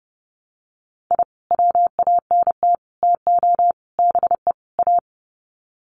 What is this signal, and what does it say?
Morse code, heard as a keyed tone, about 30 wpm, 700 Hz. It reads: I WANT TO BE A
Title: Sets of 5 Words Encore 30wpm